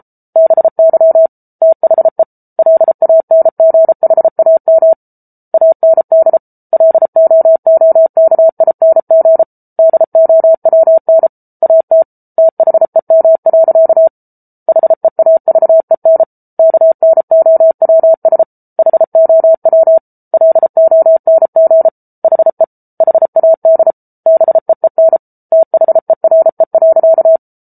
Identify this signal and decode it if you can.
Morse code, heard as a keyed tone, about 34 wpm, 650 Hz. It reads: BY THE LANGHAM AND LOOKING DOWN AT THEM. HEAVEN KNOWS HOW LONG HE HAD BEEN THERE.